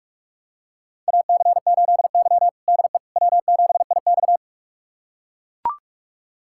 Morse code, heard as a keyed tone, about 42 wpm, 700 Hz. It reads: AC8Y DE W7IX
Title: USA Callsign DE Callsign - 4 Characters - 45wpm